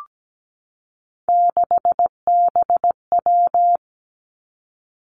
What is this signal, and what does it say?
Morse code, heard as a keyed tone, about 17 wpm, 700 Hz. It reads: 6BW
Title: Three Challenging Letter-Number Combinations 17wpm